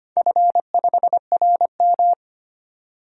Morse code, heard as a keyed tone, about 25 wpm, 700 Hz. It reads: F5RM
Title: DXCC-100 Call Signs - 4 Characters - 25wpm